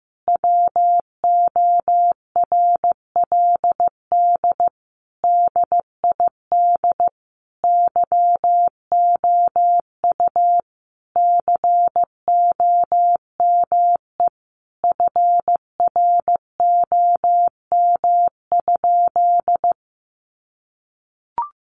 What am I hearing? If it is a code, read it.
Morse code, heard as a keyed tone, about 14 wpm, 700 Hz. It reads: WORLD DID YOU COME FROM?
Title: Sentences from Top 200 Words 15wpm